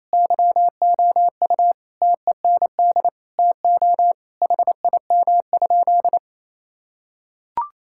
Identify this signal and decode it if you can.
Morse code, heard as a keyed tone, about 28 wpm, 700 Hz. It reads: YOU TEND TO HIM?
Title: Sentences from Top 1000 Words - Review Part 5 - 28wpm